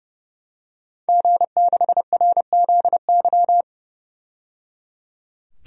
Morse code, heard as a keyed tone, about 30 wpm, 700 Hz. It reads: G6RZY
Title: DXCC-100 Call Signs - 5 Characters - 30wpm